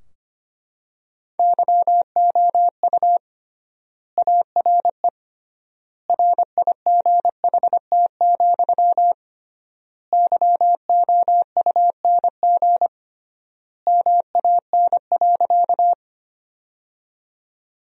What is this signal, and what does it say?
Morse code, heard as a keyed tone, about 25 wpm, 700 Hz. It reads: YOU ARE RIGHT, YOUNG MAN.
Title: 3x Word-Spacing - Sentences from Top 400 Words - 25wpm